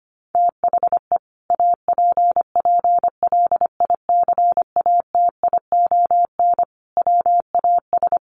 Morse code, heard as a keyed tone, about 25 wpm, 700 Hz. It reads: THE APPLICATION WAS
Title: Sentences from Top 1200 Words 25wpm